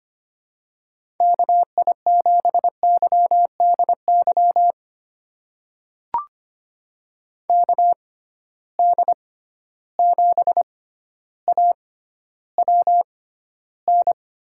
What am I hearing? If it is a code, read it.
Morse code, heard as a keyed tone, about 25 wpm, spaced as for 13 wpm, 700 Hz. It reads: KI7YDY KD7AWN
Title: USA Call Signs - 6 Characters - 6x Character Spacing - 25wpm